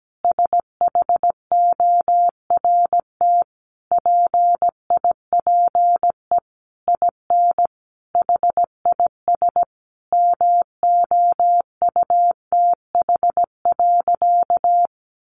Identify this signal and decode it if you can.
Morse code, heard as a keyed tone, about 17 wpm, 700 Hz. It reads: SHORT PIPE IN HIS MOUTH.
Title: Sentences from Top 1500 Words 17wpm